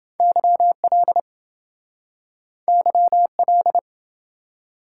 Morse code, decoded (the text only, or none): YL YL